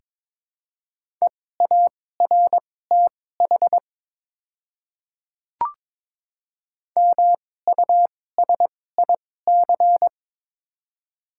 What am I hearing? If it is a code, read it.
Morse code, heard as a keyed tone, about 22 wpm, spaced as for 16 wpm, 700 Hz. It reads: EARTH MUSIC
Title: 2x Character Spacing - Top 500 Words - 22wpm